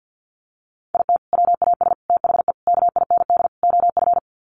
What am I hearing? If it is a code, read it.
Morse code, heard as a keyed tone, about 50 wpm, 700 Hz. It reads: IT WAS THE KIND OF
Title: Sets of 5 Words Encore2 50wpm